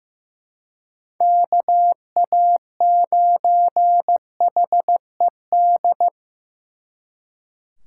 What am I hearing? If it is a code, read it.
Morse code, heard as a keyed tone, about 18 wpm, 700 Hz. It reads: KA9HED
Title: Speed Racing - USA 2x3 Call Signs 15wpm